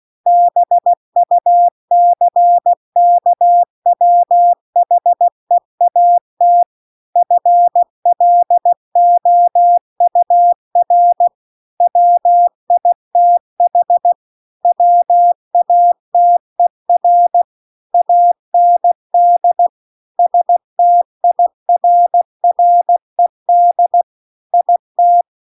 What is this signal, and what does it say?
Morse code, heard as a keyed tone, about 16 wpm, 700 Hz. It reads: BUCKWHEAT FLOUR WITH WATER AND STIRRED IT